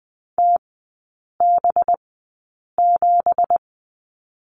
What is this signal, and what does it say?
Morse code, heard as a keyed tone, about 20 wpm, spaced as for 8 wpm, 700 Hz. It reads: TB7
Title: 218 Lesson 35e - Number 0 - Three Characters 20wpm